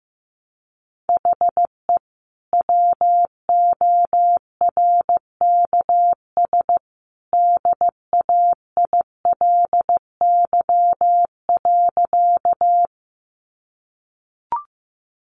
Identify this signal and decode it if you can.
Morse code, heard as a keyed tone, about 15 wpm, 700 Hz. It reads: HE WORKS DAILY.
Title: Sentences from Top 1100 Words 15wpm